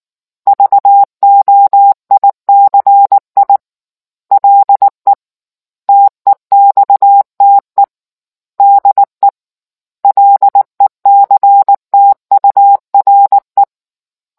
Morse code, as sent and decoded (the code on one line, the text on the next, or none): ...- --- .. -.-. .. / .-.. / - . -..- - . / -.. / .-.. -.-. - ..- .-. .
VOICI LE TEXTE DE LECTURE